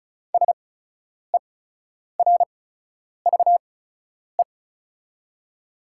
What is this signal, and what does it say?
Morse code, heard as a keyed tone, about 36 wpm, spaced as for 10 wpm, 700 Hz. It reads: SERVE
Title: 8x Character Spacing - Top 500 Words - 35wpm